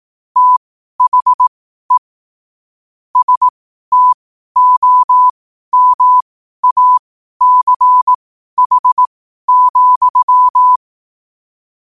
Morse code, heard as a keyed tone, about 18 wpm, spaced as for 12 wpm, 1000 Hz. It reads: THE STOMACH,
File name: qotdfiles / 12wpm